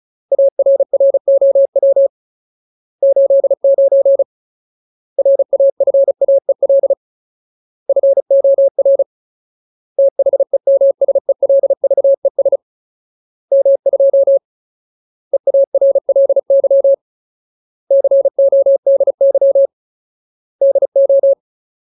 Morse code, heard as a keyed tone, about 37 wpm, 550 Hz. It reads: ARROW 89 RAFAEL FOR THEMSELVES M2 EARLY CODY DO
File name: Ultimate Word List / 35WPM Ultimate